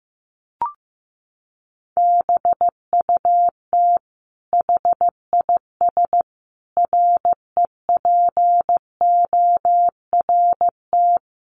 Morse code, heard as a keyed tone, about 15 wpm, 700 Hz. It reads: BUT HIS REPORT
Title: Sentences from Top 1200 Words 15wpm